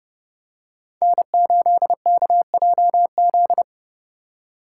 Morse code, heard as a keyed tone, about 28 wpm, 700 Hz. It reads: N8KJZ
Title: Code Groups - 5 Letter:Numbers - 30wpm